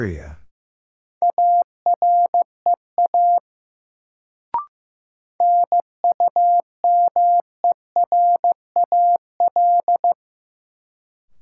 Morse code, heard as a keyed tone, about 15 wpm, 700 Hz. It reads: AREA NUMERAL
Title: Sentences from Top 1000 Words - Review Part 2 - New Word Drill - 15wpm